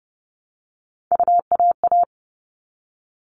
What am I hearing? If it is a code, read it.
Morse code, heard as a keyed tone, about 30 wpm, 700 Hz. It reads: UAA
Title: All Cut Numbers - 3-Digit 30wpm